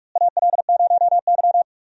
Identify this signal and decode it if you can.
Morse code, heard as a keyed tone, about 43 wpm, 700 Hz. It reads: AL0Y
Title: USA Call Signs - 4 Characters - Repeated 3x - 45wpm